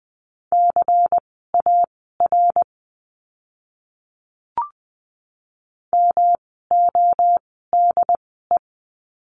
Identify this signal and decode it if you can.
Morse code, heard as a keyed tone, about 20 wpm, spaced as for 15 wpm, 700 Hz. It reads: CAR MODE
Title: 108 Lesson 18f - Letter m - Words 20wpm